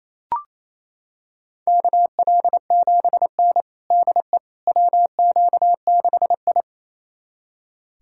Morse code, heard as a keyed tone, about 27 wpm, 700 Hz. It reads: KL7N DE WQ6I
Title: USA Callsign DE Callsign - 4 Characters - 28wpm